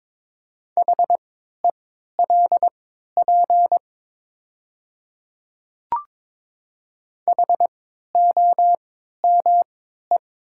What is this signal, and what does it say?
Morse code, heard as a keyed tone, about 22 wpm, spaced as for 13 wpm, 690 Hz. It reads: HELP HOME
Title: 3x Character Spacing - Top 300 Words - 22wpm